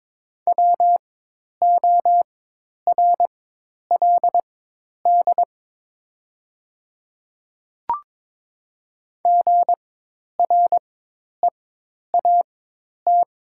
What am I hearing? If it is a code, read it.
Morse code, heard as a keyed tone, about 22 wpm, spaced as for 10 wpm, 700 Hz. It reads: WORLD GREAT
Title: 4x Character Spacing - Top 500 Words - 22wpm